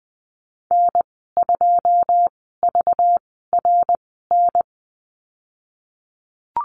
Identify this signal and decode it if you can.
Morse code, heard as a keyed tone, about 20 wpm, spaced as for 14 wpm, 700 Hz. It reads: N2VRN
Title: 231 Lesson 37g - Letter x - Call Signs 20wpm